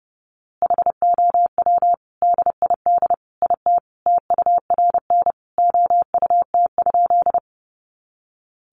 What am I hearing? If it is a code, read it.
Morse code, heard as a keyed tone, about 28 wpm, 700 Hz. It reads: HOW DID IT TURN OUT?
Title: A Sound Of Thunder 30wpm